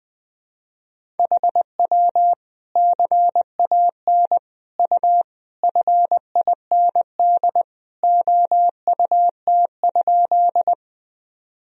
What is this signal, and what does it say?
Morse code, heard as a keyed tone, about 20 wpm, 700 Hz. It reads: HW CAN U FIND OUT?
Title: Sentences from Top 200 Words and Abbreviations 20wpm